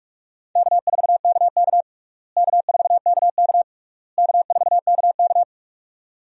Morse code, heard as a keyed tone, about 43 wpm, 700 Hz. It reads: K4KK K4KK K4KK
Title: USA Call Signs - 4 Characters - Repeated 3x - 45wpm